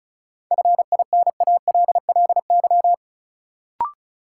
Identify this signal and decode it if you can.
Morse code, heard as a keyed tone, about 34 wpm, 700 Hz. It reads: FINALLY
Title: Sentences from Top 1000 Words - New Word Drill - 35wpm